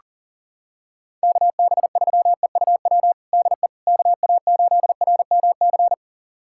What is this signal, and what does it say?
Morse code, heard as a keyed tone, about 37 wpm, 700 Hz. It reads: KB3EUW DE KA8RMC
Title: USA Callsign DE Callsign - 6 Characters - 40wpm